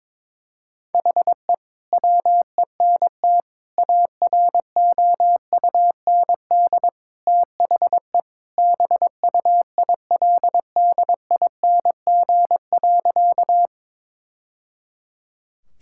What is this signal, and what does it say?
Morse code, heard as a keyed tone, about 22 wpm, 700 Hz. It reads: HE WENT AROUND THE BUILDING.
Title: Sentences from Top 1000 Words - Review Part 5 - 22wpm